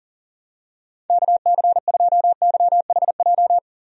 Morse code, heard as a keyed tone, about 40 wpm, 700 Hz. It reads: KC2YHJ
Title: USA Call Signs - 6 Characters - Repeated 3x - 40wpm